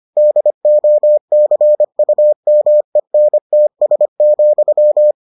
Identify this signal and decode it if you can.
Morse code, heard as a keyed tone, about 25 wpm, 600 Hz. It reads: DOCUMENTS,